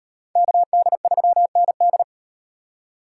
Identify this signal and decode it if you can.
Morse code, heard as a keyed tone, about 38 wpm, 700 Hz. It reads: KD3ND